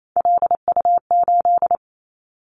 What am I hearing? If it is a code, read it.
Morse code, heard as a keyed tone, about 28 wpm, 700 Hz. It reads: LU8
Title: Callsign Prefixes for Top 100 DXCC - 28wpm